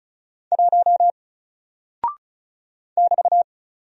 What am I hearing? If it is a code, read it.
Morse code, heard as a keyed tone, about 36 wpm, 700 Hz. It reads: 1 <BT>
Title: Single Character (Letter:Number:Punctuation:Prosign) Alt 35wpm